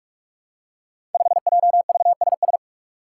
Morse code, heard as a keyed tone, about 42 wpm, 700 Hz. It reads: 5JVSS
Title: Five Challenging Letter-Number Combinations 45wpm